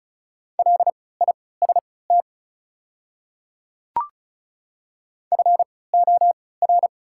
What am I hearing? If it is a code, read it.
Morse code, heard as a keyed tone, about 36 wpm, spaced as for 20 wpm, 700 Hz. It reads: LIST FOR
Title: 3x Character Spacing - Top 500 Words - 35wpm